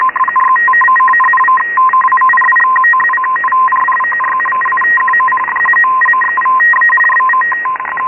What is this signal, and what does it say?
Two-tone teleprinter signal, about 850 Hz shift.
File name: Radio / SDR Audio / gqrx